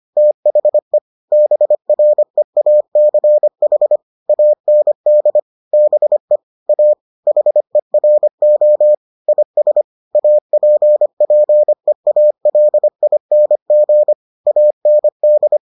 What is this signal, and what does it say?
Morse code, heard as a keyed tone, about 25 wpm, 600 Hz. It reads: THE BREACH AND BE A HERO IS APPEALING AND